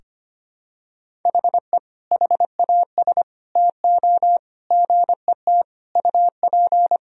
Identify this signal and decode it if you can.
Morse code, heard as a keyed tone, about 25 wpm, 700 Hz. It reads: HE HAS TO GET UP
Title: Sentences from Top 1000 Words - Review Part 2 - 25wpm